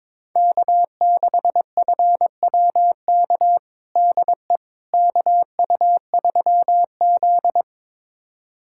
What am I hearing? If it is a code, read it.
Morse code, heard as a keyed tone, about 22 wpm, 700 Hz. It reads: K6FWK DE KU3Z
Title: USA Callsign DE Callsign 22wpm